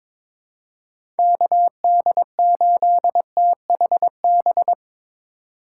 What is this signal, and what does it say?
Morse code, heard as a keyed tone, about 22 wpm, 700 Hz. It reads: KD8THB
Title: USA Call Signs - 6 Characters - 3x Character Spacing - 22wpm